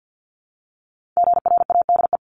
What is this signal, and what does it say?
Morse code, heard as a keyed tone, about 50 wpm, 690 Hz. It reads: GRADE
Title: Multi-letter Phonemes Repeated 3x 50wpm